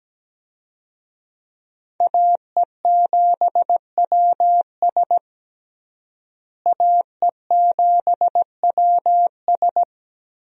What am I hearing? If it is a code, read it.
Morse code, heard as a keyed tone, about 17 wpm, 700 Hz. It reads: AE7WS AE7WS
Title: USA Call Signs - 5 Characters - Repeated 2x - 17wpm